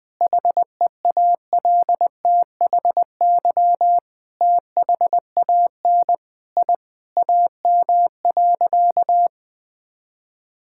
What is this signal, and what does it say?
Morse code, heard as a keyed tone, about 20 wpm, 700 Hz. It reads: HEALTHY THAN I AM.
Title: Sentences from Top 1000 Words - Review Part 5 - 20wpm